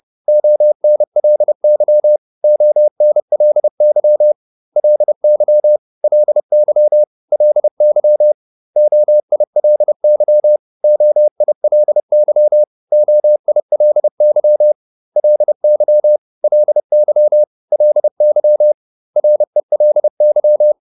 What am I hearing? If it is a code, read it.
Morse code, heard as a keyed tone, about 30 wpm, 600 Hz. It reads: ONLY ONLY LY LY LY OILY OILY OILY LY LY LY RELY